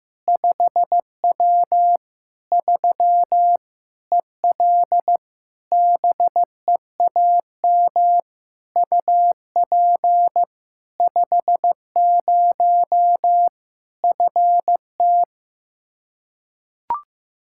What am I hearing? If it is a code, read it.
Morse code, heard as a keyed tone, about 15 wpm, 700 Hz. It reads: 5W 3 EL BEAM UP 50 FT
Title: Practice QSO - Exchange 2 - Info - 15wpm